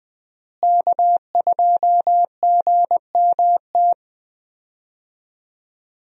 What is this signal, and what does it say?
Morse code, heard as a keyed tone, about 20 wpm, 700 Hz. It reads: K2GMT
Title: USA Call Signs - 5 Characters - Repeated 3x - 20wpm